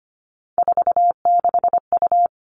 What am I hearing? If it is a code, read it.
Morse code, heard as a keyed tone, about 25 wpm, 700 Hz. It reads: 46U
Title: All Cut Numbers - 3-Digit 25wpm